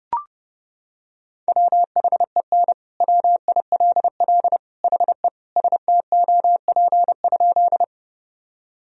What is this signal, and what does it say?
Morse code, heard as a keyed tone, about 32 wpm, 700 Hz. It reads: WHEN WILL HE STOP?
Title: Sentences from Top 1000 Words - Review Part 1 - 30wpm